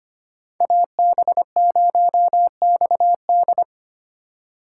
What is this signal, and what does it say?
Morse code, heard as a keyed tone, about 25 wpm, 700 Hz. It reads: AB0XD
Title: USA Call Signs - 5 Characters - 2x Character Spacing - 25wpm